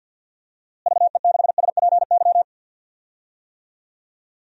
Morse code, heard as a keyed tone, about 47 wpm, 700 Hz. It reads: VE6SPY
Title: DXCC-100 Call Signs - 6 Characters - Repeated 2x - 50wpm